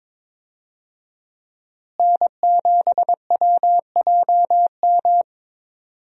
Morse code, heard as a keyed tone, about 23 wpm, 700 Hz. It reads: N7WJM
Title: USA Call Signs - 5 Characters - Repeated 3x - 22wpm